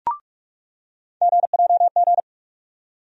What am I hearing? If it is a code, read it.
Morse code, heard as a keyed tone, about 46 wpm, 700 Hz. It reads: GJG
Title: USA Call Sign Postfixes - 3 Letters 45wpm